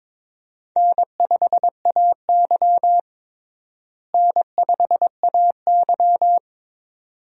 Morse code, heard as a keyed tone, about 23 wpm, 700 Hz. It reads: N5AY N5AY
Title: USA Call Signs - 4 Characters - Repeated 2x - 22wpm